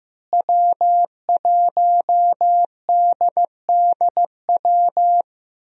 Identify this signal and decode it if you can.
Morse code, heard as a keyed tone, about 15 wpm, 700 Hz. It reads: W1DDW
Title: USA Call Signs - 5 Characters - Repeated 2x - 15wpm